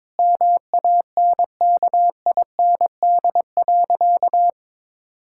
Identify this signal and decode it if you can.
Morse code, heard as a keyed tone, about 22 wpm, 700 Hz. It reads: MANKIND.
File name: Difficult Sentences with Hints - 22wpm